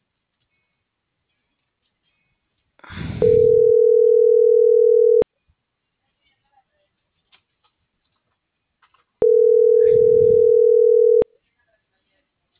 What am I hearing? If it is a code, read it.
Ringback tone (North America), 440 + 480 Hz.